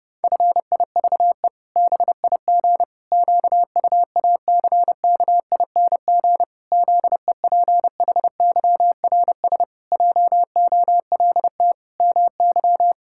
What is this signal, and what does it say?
Morse code, heard as a keyed tone, about 30 wpm, 700 Hz. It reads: FIVE BIG QUACKING ZEPHYRS JOLT MY
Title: Pangram Challenge v2 - 30wpm